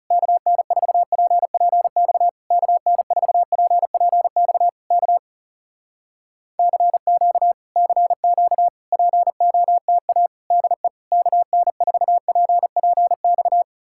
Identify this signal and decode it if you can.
Morse code, heard as a keyed tone, about 40 wpm, 700 Hz. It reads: KN4PPX KN4PPX K CQ CQ POTA DE KN4PPX